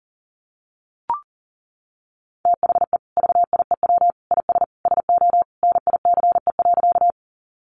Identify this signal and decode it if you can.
Morse code, heard as a keyed tone, about 37 wpm, 700 Hz. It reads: THE VIEW IS SO NICE.
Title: Additional Easy Phrases 40wpm